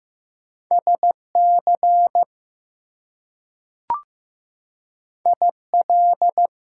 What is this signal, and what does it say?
Morse code, heard as a keyed tone, about 15 wpm, 700 Hz. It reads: SC IL
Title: US States and Canadian Provinces 15wpm